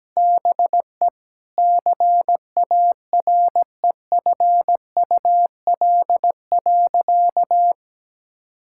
Morse code, heard as a keyed tone, about 17 wpm, 700 Hz. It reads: BE CAREFUL.
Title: Sentences from Top 1600 Words 17wpm